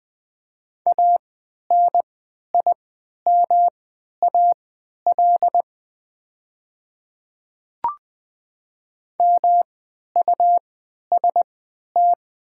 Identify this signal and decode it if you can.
Morse code, heard as a keyed tone, about 20 wpm, spaced as for 11 wpm, 700 Hz. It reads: ANIMAL MUST
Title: 3x Character Spacing - Top 200 Words - 20wpm